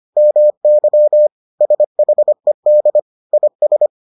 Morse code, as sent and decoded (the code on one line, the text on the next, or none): -- -.-- / ... .... -.. / .. ...
MY SHED IS